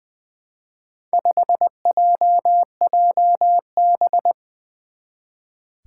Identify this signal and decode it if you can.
Morse code, heard as a keyed tone, about 20 wpm, 700 Hz. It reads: 5JJB
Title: Four Challenging Letter-Number Combinations 20wpm